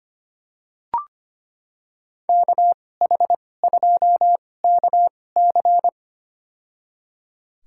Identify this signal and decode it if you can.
Morse code, heard as a keyed tone, about 25 wpm, spaced as for 18 wpm, 700 Hz. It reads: KH2KC
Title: USA Call Signs - 5 Characters - 2x Character Spacing - 25wpm